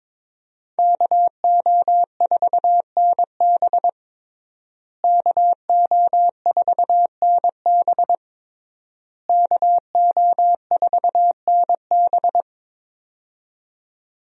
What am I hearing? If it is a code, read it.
Morse code, heard as a keyed tone, about 22 wpm, 700 Hz. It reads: KO4NB KO4NB KO4NB
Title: USA Call Signs - 5 Characters - Repeated 3x - 22wpm